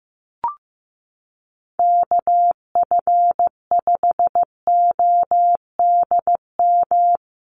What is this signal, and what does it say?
Morse code, heard as a keyed tone, about 14 wpm, 700 Hz. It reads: KF5ODM
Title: USA Call Signs Repeated 3x 15wpm